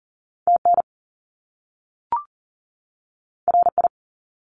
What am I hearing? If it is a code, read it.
Morse code, heard as a keyed tone, about 40 wpm, 700 Hz. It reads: TN RI
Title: US States 40wpm